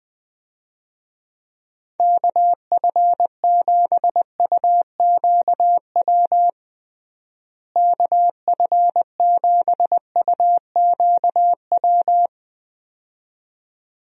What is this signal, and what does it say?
Morse code, heard as a keyed tone, about 21 wpm, 700 Hz. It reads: KF7UQW KF7UQW
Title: USA Call Signs - 6 Characters - Repeated 2x - 20wpm